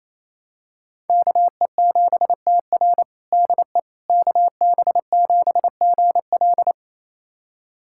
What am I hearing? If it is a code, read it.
Morse code, heard as a keyed tone, about 28 wpm, 700 Hz. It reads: KE7TR DE KB7GL
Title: USA Callsign DE Callsign - 5 Characters - 28wpm